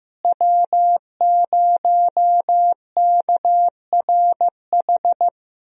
Morse code, heard as a keyed tone, about 15 wpm, 700 Hz. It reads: W0KRH
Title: USA Call Signs - 5 Characters - Repeated 2x - 15wpm